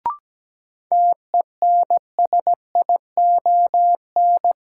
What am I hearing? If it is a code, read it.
Morse code, heard as a keyed tone, about 17 wpm, 700 Hz. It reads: TENSION
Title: Sentences from Top 1600 Words - New Word Drill - 17wpm